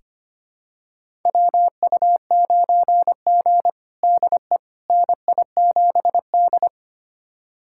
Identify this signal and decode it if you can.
Morse code, heard as a keyed tone, about 25 wpm, 700 Hz. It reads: WU9G DE NI7D
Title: USA Callsign DE Callsign - 4 Characters - 25wpm